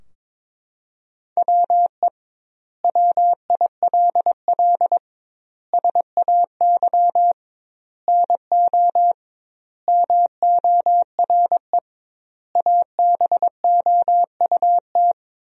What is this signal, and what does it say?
Morse code, heard as a keyed tone, about 22 wpm, 700 Hz. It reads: WE WILL SAY NO MORE ABOUT
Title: 2x Word-Spacing - Sentences from Top 200 Words - 22wpm